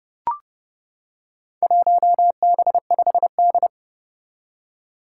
Morse code, heard as a keyed tone, about 30 wpm, 700 Hz. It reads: 1B5D
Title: Four Challenging Letter-Number Combinations 30wpm